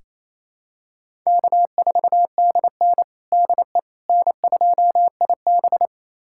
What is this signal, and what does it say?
Morse code, heard as a keyed tone, about 28 wpm, 700 Hz. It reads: K4DN DE N2IB
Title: USA Callsign DE Callsign - 4 Characters - 28wpm